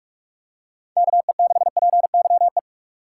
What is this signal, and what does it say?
Morse code, heard as a keyed tone, about 46 wpm, 700 Hz. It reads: KE6PYE
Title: USA Call Signs - 2x Character Spacing - 45wpm